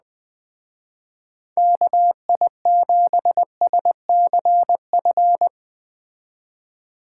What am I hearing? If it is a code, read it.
Morse code, heard as a keyed tone, about 19 wpm, 700 Hz. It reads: KI7SCF